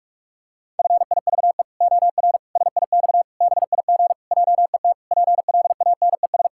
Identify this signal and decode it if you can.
Morse code, heard as a keyed tone, about 46 wpm, 700 Hz. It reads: FIVE OR SIX BIG JET PLANES
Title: Pangram Challenge v2 - 45wpm